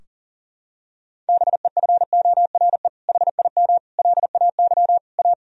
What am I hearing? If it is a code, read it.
Morse code, heard as a keyed tone, about 37 wpm, 700 Hz. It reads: BEFORE HIM LAY A
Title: Sentences from Top 1000 Words - Review Part 2 - 40wpm